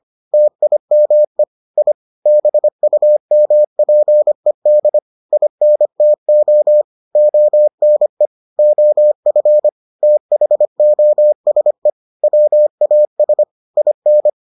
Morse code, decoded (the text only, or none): TIME I BUMPED INTO ONE OF THOSE WAS IN